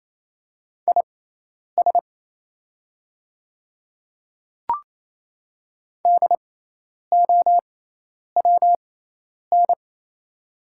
Morse code, heard as a keyed tone, about 28 wpm, spaced as for 10 wpm, 690 Hz. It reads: IS DOWN
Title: 6x Character Spacing - Top 100 Words - 28wpm